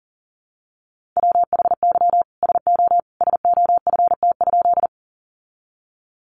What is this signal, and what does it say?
Morse code, heard as a keyed tone, about 40 wpm, 700 Hz. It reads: WHY SO SOFT?
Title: Sentences from Top 1500 Words 40wpm